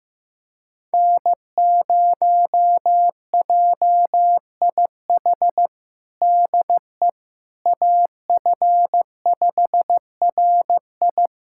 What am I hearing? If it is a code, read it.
Morse code, heard as a keyed tone, about 15 wpm, 700 Hz. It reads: N0JIH DE AF5RI
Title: USA Callsign DE Callsign - 5 Characters - 15wpm